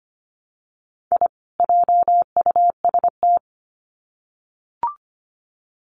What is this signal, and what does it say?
Morse code, heard as a keyed tone, about 25 wpm, 700 Hz. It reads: I JUST
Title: Sets of 2 Words 25wpm